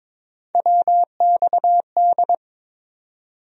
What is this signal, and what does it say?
Morse code, heard as a keyed tone, about 22 wpm, 700 Hz. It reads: WXD
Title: USA Call Sign Postfixes - 3 Letters 22wpm